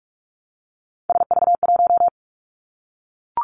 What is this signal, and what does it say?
Morse code, heard as a keyed tone, about 46 wpm, 700 Hz. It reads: SV1
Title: Callsign Prefixes for Top 50 DXCC - 45wpm